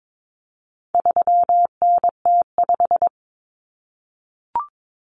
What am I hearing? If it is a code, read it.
Morse code, heard as a keyed tone, about 22 wpm, 700 Hz. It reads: 3NT5
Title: Common Cut Numbers - 4-Digit 22wpm